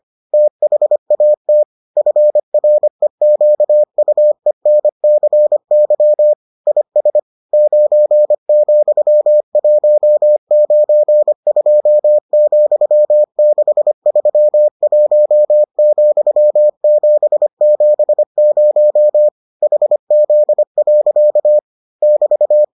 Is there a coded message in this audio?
Yes: Morse code from 0.3 to 22.6 s, heard as a keyed tone, about 25 wpm, 600 Hz. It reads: THAT FREQUENCY IS 9,192,631,770 HZ. <BT>